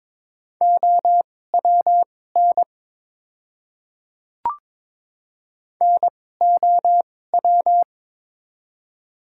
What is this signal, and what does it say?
Morse code, heard as a keyed tone, about 22 wpm, spaced as for 16 wpm, 700 Hz. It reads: OWN NOW
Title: 2x Character Spacing - Top 300 Words - 22wpm